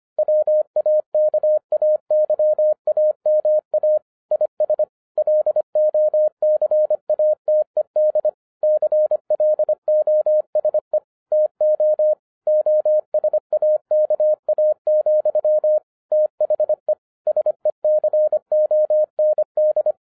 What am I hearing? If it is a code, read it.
Morse code, heard as a keyed tone, about 25 wpm, 610 Hz. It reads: WAKAYAMA IS LOCATED CLOSE TO OSAKA, THE SECOND
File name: 2011-10-mp3 / n04-JUST MOVED 25wpm